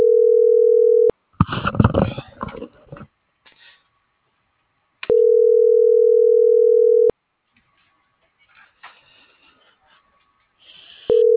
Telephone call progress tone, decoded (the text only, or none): ringback tone